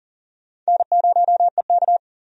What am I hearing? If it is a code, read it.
Morse code, heard as a keyed tone, about 40 wpm, 700 Hz. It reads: N0EK
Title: USA Call Signs - 4 Characters - 40wpm